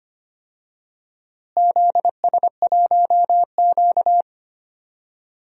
Morse code, heard as a keyed tone, about 25 wpm, 700 Hz. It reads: ZS1Q
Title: DXCC-100 Call Signs - 4 Characters - Repeated 3x - 25wpm